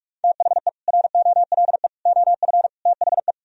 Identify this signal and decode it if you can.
Morse code, heard as a keyed tone, about 58 wpm, 700 Hz. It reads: THE ROLE OF THE